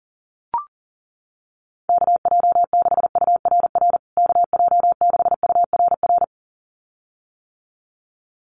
Morse code, heard as a keyed tone, about 40 wpm, 700 Hz. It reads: KJ6URR KJ6URR
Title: USA Call Signs Repeated 2x 40wpm